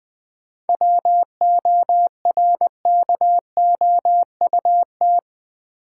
Morse code, heard as a keyed tone, about 20 wpm, 700 Hz. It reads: WORKOUT
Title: 7 Letter Words - First 3 Letters Given - 20wpm